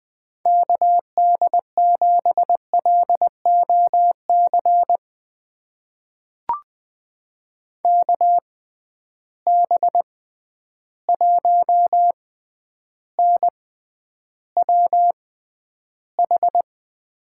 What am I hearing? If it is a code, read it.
Morse code, heard as a keyed tone, about 21 wpm, spaced as for 10 wpm, 700 Hz. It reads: KD7LOC KB1NWH
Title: USA Call Signs - 6 Characters - 6x Character Spacing - 20wpm